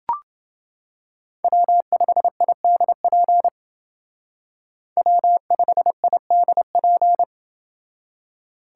Morse code, heard as a keyed tone, about 28 wpm, 700 Hz. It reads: W5IDP W5IDP